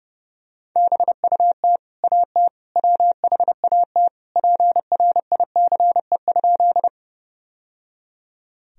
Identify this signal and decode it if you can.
Morse code, heard as a keyed tone, about 30 wpm, 700 Hz. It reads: BUT AT WHAT PRICE?
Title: Sentences from Top 1000 Words - Review Part 4 - 30wpm